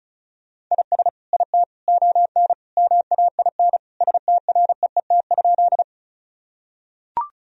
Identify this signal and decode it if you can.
Morse code, heard as a keyed tone, about 34 wpm, 700 Hz. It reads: IS IT ON MAIN STREET?
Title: Sentences from Top 1000 Words - Review Part 2 - 35wpm